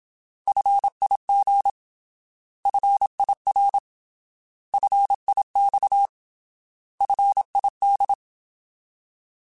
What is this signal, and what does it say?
Morse code, heard as a keyed tone, about 27 wpm, 800 Hz. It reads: FIG FIR FIX FID
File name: Alphabetical / Alphabetical mp3s / Fs @25